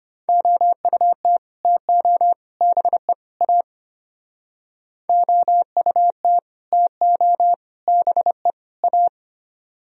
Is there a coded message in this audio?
Yes: Morse code from 0.3 to 9.1 s, heard as a keyed tone, about 32 wpm, 700 Hz. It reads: OUT TO BE A OUT TO BE A